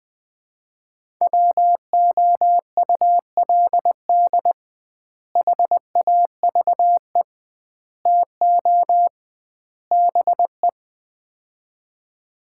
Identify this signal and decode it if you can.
Morse code, heard as a keyed tone, about 20 wpm, 700 Hz. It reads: WOULD HAVE TO BE
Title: 2x Word-Spacing - Sets of 4 Words - 20wpm